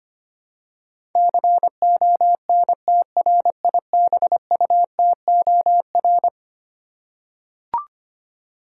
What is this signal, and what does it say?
Morse code, heard as a keyed tone, about 25 wpm, 700 Hz. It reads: CONTRIBUTOR
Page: morse-code-ninja-practice-25wpm